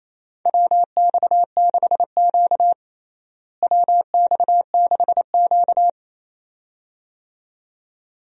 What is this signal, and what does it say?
Morse code, heard as a keyed tone, about 28 wpm, 700 Hz. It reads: WX6Q WX6Q
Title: USA Call Signs - 4 Characters - Repeated 2x - 28wpm